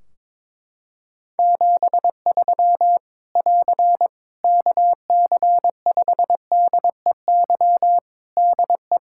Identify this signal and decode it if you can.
Morse code, heard as a keyed tone, about 22 wpm, 700 Hz. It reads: 73 <AR> KC5DEY DE
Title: Practice QSO - Exchange 4 - Ending - 22wpm